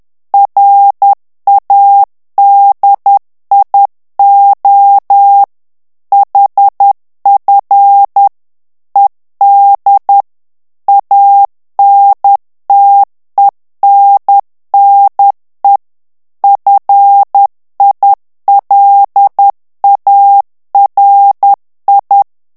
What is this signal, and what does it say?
Morse code, heard as a keyed tone, about 10 wpm, 800 Hz. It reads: RADIO HF ED ANTENNE FILARI
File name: morse mp3 / ITALIEN texte clair 1 vitesse 10 mots minute